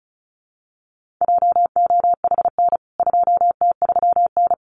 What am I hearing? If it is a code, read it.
Morse code, heard as a keyed tone, about 34 wpm, 700 Hz. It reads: JOHN 2T3N
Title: CWT Exchange 35wpm